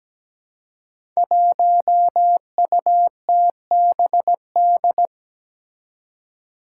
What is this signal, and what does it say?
Morse code, heard as a keyed tone, about 17 wpm, 700 Hz. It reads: 1UTBD
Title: Five Challenging Letter-Number Combinations 17wpm